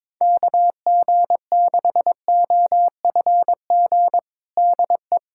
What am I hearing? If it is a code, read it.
Morse code, heard as a keyed tone, about 22 wpm, 700 Hz. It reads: KG6OFG DE
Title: USA Callsign DE Callsign 22wpm